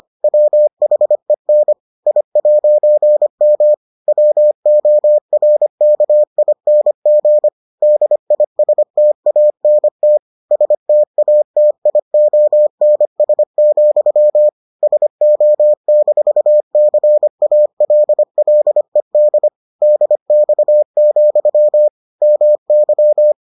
Morse code, heard as a keyed tone, about 25 wpm, 600 Hz. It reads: WHEN I'M WORKING DISTANT STATIONS, SO-CALLED DX, MY